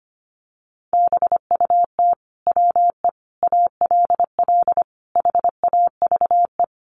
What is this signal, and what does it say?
Morse code, heard as a keyed tone, about 25 wpm, 700 Hz. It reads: BUT WE ALL HAVE
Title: Sentences from Top 1300 Words 25wpm